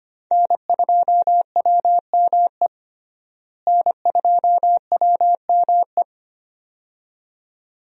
Morse code, heard as a keyed tone, about 25 wpm, 700 Hz. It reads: N2WME N2WME
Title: USA Call Signs - 5 Characters - Repeated 2x - 25wpm